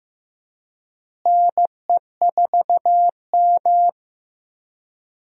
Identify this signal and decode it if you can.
Morse code, heard as a keyed tone, about 15 wpm, 700 Hz. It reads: NE4M
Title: USA Call Signs - 4 Characters - Repeated 3x - 15wpm